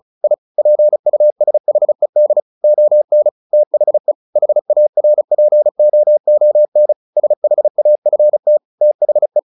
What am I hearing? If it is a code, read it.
Morse code, heard as a keyed tone, about 34 wpm, 600 Hz. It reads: I PUSHED ON THE HARPOON SHAFT THE
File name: the old man and the sea 2 35wpm